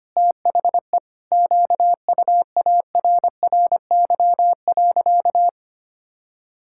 Morse code, heard as a keyed tone, about 25 wpm, 700 Hz. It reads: THE QUARRY.
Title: Pangram Challenge v2 - 25wpm